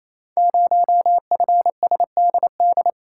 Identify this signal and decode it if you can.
Morse code, heard as a keyed tone, about 29 wpm, 700 Hz. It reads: 0FSDD
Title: Code Groups - 5 Letter:Numbers - 28wpm